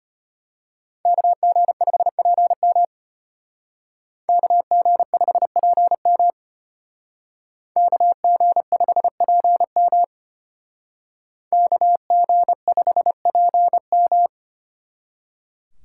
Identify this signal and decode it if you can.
Morse code, heard as a keyed tone, about 32 wpm, 700 Hz. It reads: KG5PM KG5PM KG5PM KG5PM